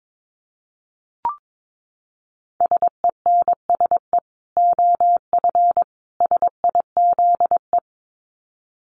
Morse code, heard as a keyed tone, about 22 wpm, 700 Hz. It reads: SENSE OF SIZE
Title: Dit Challenge 22wpm